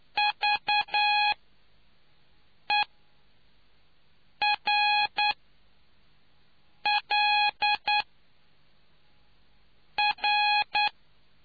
Morse code, heard as a keyed tone, about 9 wpm, spaced as for 4 wpm, 830 Hz. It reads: VERLR